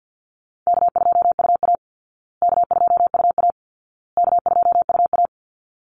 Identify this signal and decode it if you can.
Morse code, heard as a keyed tone, about 48 wpm, 700 Hz. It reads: K2UA K2UA K2UA
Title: Easy Call Signs Repeated 3x - 50wpm